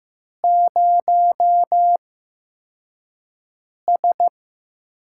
Morse code, heard as a keyed tone, about 15 wpm, spaced as for 4 wpm, 700 Hz. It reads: 0S
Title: USA Call Signs - 4 Characters - 8x Character Spacing - 15wpm